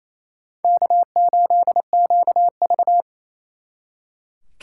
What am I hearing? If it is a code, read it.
Morse code, heard as a keyed tone, about 36 wpm, 700 Hz. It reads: K8QV